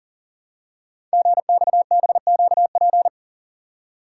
Morse code, heard as a keyed tone, about 41 wpm, 700 Hz. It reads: GXBQP
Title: Code Groups - 5 Letters - 40wpm